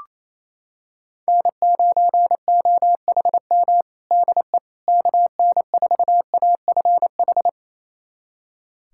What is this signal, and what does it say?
Morse code, heard as a keyed tone, about 27 wpm, 700 Hz. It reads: N9OHM DE KN4AFH
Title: USA Callsign DE Callsign 28wpm